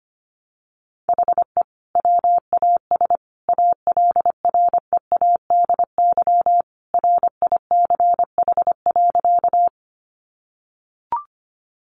Morse code, heard as a keyed tone, about 25 wpm, 700 Hz. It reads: HE WAS ALREADY RICH.
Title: Sentences from Top 1100 Words 25wpm